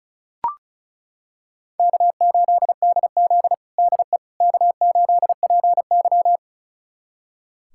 Morse code, heard as a keyed tone, about 34 wpm, 700 Hz. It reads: K8DZ DE K8PY
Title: USA Callsign DE Callsign - 4 Characters - 35wpm